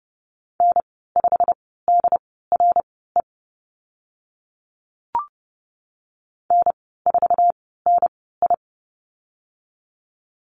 Morse code, heard as a keyed tone, about 30 wpm, spaced as for 17 wpm, 700 Hz. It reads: N5DRE N4NI
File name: Lesson 15g - Number 5 - Call Signs 30wpm